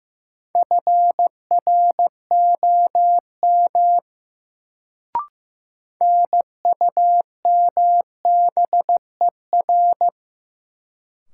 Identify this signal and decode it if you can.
Morse code, heard as a keyed tone, about 15 wpm, 700 Hz. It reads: FROM NUMBER
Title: Sentences from Top 100 Words - New Word Drill - 15wpm